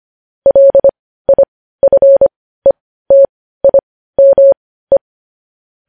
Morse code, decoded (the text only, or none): LIFETIME